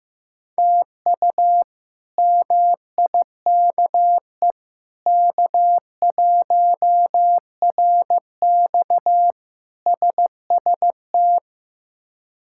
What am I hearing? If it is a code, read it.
Morse code, heard as a keyed tone, about 15 wpm, 700 Hz. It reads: TU MIKE K1RX SST
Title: K1USN SST QSO Practice - 15wpm